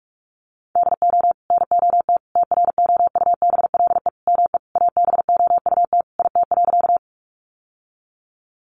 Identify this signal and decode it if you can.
Morse code, heard as a keyed tone, about 44 wpm, 700 Hz. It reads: DO NOT TROUBLE ME ABOUT IT.
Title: Sentences from Top 1200 Words 45wpm